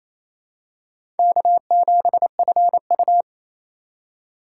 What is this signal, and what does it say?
Morse code, heard as a keyed tone, about 28 wpm, 700 Hz. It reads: K7FU
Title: USA Call Signs - 4 Characters - 4x Character Spacing - 28wpm